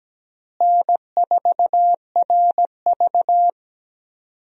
Morse code, heard as a keyed tone, about 17 wpm, 700 Hz. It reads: N4RV
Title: USA Call Signs - 4 Characters - Repeated 3x - 17wpm